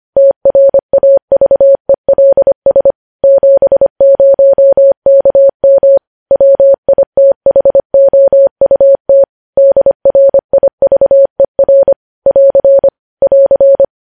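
Morse code, heard as a keyed tone, about 25 wpm, 550 Hz. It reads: TRAVELS 70KM WITHOUT DRIVER <AR> <AR>
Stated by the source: News headlines sent at 25.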